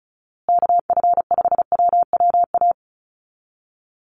Morse code, heard as a keyed tone, about 34 wpm, 700 Hz. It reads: KF5WWA
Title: USA Call Signs Repeated 3x 35wpm